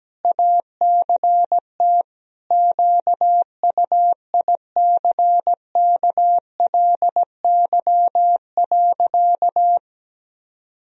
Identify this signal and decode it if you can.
Morse code, heard as a keyed tone, about 17 wpm, 700 Hz. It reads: ACT QUICKLY.
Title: Sentences from Top 1000 Words - Review Part 4 - 17wpm